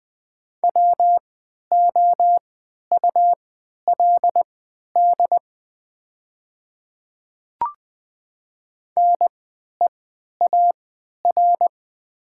Morse code, heard as a keyed tone, about 20 wpm, spaced as for 11 wpm, 700 Hz. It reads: WOULD NEAR
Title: 3x Character Spacing - Top 500 Words - 20wpm